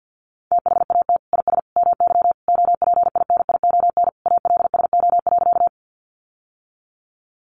Morse code, heard as a keyed tone, about 50 wpm, 700 Hz. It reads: THAT IS MY OPINION ALSO.
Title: Sentences from Top 1200 Words 50wpm